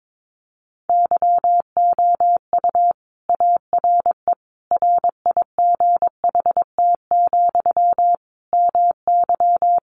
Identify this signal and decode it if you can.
Morse code, heard as a keyed tone, about 22 wpm, 700 Hz. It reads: YOU ARE RIGHT, MY
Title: Sentences from Top 400 Words 22wpm